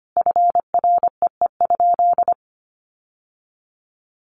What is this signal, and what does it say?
Morse code, heard as a keyed tone, about 25 wpm, 700 Hz. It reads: FREE?
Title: Sentences from Top 1100 Words 25wpm